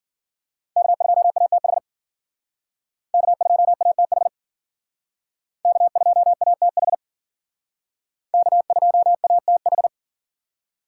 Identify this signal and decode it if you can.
Morse code, heard as a keyed tone, about 51 wpm, 700 Hz. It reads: K2ATH K2ATH K2ATH K2ATH